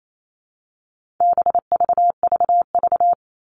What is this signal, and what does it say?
Morse code, heard as a keyed tone, about 28 wpm, 700 Hz. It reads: BVVV
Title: All Cut Numbers - 4-Digit 28wpm